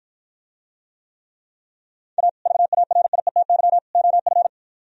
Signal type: Morse code

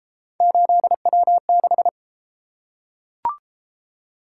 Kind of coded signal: Morse code

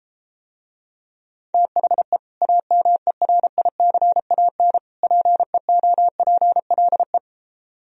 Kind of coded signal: Morse code

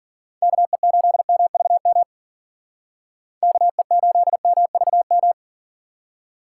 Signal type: Morse code

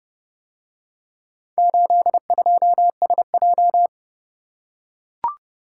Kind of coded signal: Morse code

